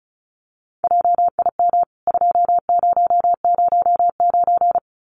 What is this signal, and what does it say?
Morse code, heard as a keyed tone, about 34 wpm, 700 Hz. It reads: JIM 2009
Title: CWT Exchange 35wpm